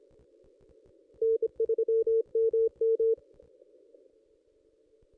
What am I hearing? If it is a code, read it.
Morse code, heard as a keyed tone, about 26 wpm, 450 Hz. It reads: N3MM
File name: contests / ss-cw-k5tr-op-n5rz -1800